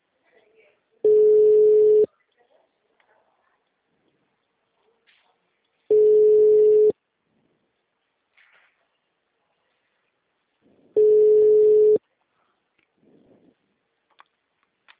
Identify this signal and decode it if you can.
Ringback tone (Europe, 425 Hz), with speech.